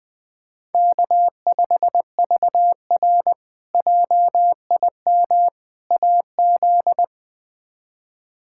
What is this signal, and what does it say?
Morse code, heard as a keyed tone, about 21 wpm, 700 Hz. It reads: K5VR JIM AZ
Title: K1USN SST QSO Practice - 20wpm